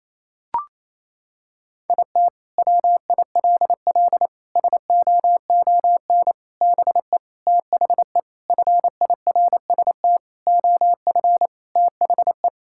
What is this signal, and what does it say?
Morse code, heard as a keyed tone, about 28 wpm, 700 Hz. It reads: IT WILL SOON BE THE FIRST OF THE
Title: Sentences from Top 1000 Words - Review Part 5 - 28wpm